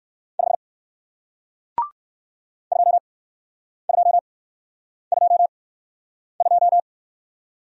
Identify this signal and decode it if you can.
Morse code, heard as a keyed tone, about 56 wpm, spaced as for 9 wpm, 700 Hz. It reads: 5 2222
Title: Speed-Racing Single Letter-Number 45wpm